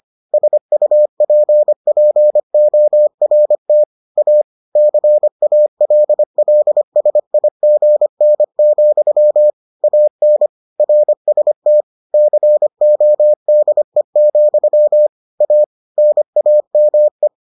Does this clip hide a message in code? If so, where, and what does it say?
Yes: Morse code from 0.3 to 17.3 s, heard as a keyed tone, about 25 wpm, 600 Hz. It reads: SUPPORT A CALLSIGN, AN RST CODE, A NAME